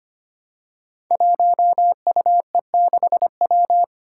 Morse code, heard as a keyed tone, about 25 wpm, 700 Hz. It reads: 1UE6W
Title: Five Challenging Letter-Number Combinations 25wpm